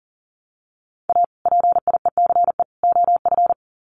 Morse code, heard as a keyed tone, about 40 wpm, 700 Hz. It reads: A PIECE OF
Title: Sets of 3 Words 40wpm